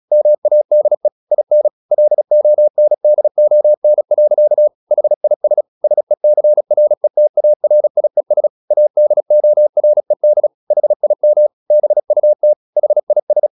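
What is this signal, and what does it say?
Morse code, heard as a keyed tone, about 36 wpm, 600 Hz. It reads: MADE IN LONDON. HIS SECRETARIES ADORED HIM BUT HIS
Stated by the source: Morse practice at 36 wpm, text of Ken Follett's novel Never, from CW med Gnister.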